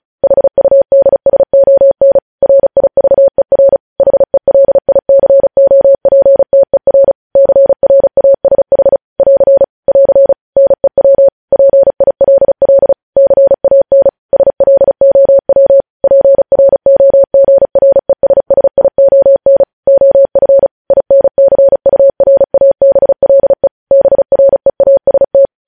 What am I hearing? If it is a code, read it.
Morse code, heard as a keyed tone, about 36 wpm, 550 Hz. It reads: HUDSON RIVER HELICOPTER CRASH <AR> <AR> NEW PILL CAN SLOW PROGRESSION OF INCURABLE BREAST
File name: podcast / News headlines-1x-35x35